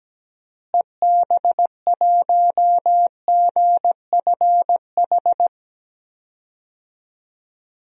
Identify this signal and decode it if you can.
Morse code, heard as a keyed tone, about 17 wpm, 700 Hz. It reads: EB1GFH
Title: DXCC-100 Call Signs - 6 Characters - Repeated 3x - 17wpm